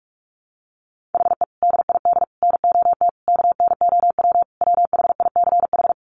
Morse code, heard as a keyed tone, about 44 wpm, 700 Hz. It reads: HE DID NOT KNOW WHICH
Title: Sentences from Top 1300 Words 45wpm